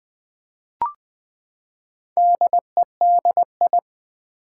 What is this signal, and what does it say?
Morse code, heard as a keyed tone, about 20 wpm, 700 Hz. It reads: DEDI
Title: Build A Word - 4 Letters At A Time - 10-Letter Words - 20wpm